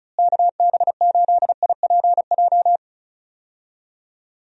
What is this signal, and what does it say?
Morse code, heard as a keyed tone, about 36 wpm, 700 Hz. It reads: KB8IPJ
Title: USA Call Signs - 6 Characters - Repeated 3x - 35wpm